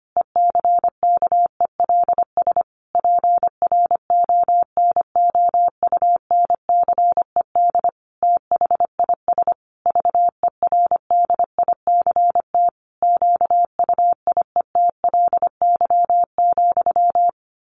Morse code, heard as a keyed tone, about 25 wpm, 700 Hz. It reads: ECKELS PRONOUNCED THIS VERDICT QUIETLY,
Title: A Sound Of Thunder 25wpm